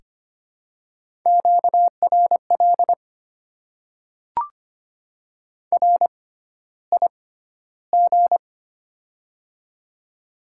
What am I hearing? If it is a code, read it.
Morse code, heard as a keyed tone, about 25 wpm, spaced as for 13 wpm, 690 Hz. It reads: QRL RIG
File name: Top 20 QSO Elements - 6x Character Spacing - 25wpm